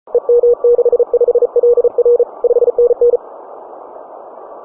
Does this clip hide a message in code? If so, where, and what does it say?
Yes: Morse code from 0.1 to 3.2 s, heard as a keyed tone, about 33 wpm, 480 Hz, with radio noise. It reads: EM65LR 5NN